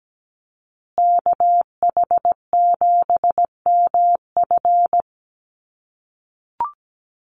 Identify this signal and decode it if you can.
Morse code, heard as a keyed tone, about 17 wpm, 700 Hz. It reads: KH7MF
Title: USA Call Signs Repeated 2x 17wpm